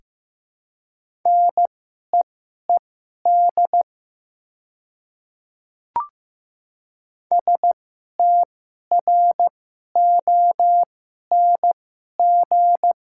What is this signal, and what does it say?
Morse code, heard as a keyed tone, about 15 wpm, spaced as for 11 wpm, 700 Hz. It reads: NEED STRONG
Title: 2x Character Spacing - Top 500 Words - 15wpm